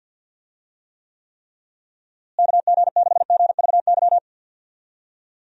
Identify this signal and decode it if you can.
Morse code, heard as a keyed tone, about 46 wpm, 700 Hz. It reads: KG6GVY
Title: USA Call Signs - 6 Characters - Repeated 3x - 50wpm